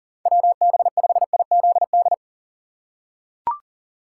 Morse code, heard as a keyed tone, about 40 wpm, 700 Hz. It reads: WB5IZD
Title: USA Call Signs - 6 Characters - 2x Character Spacing - 40wpm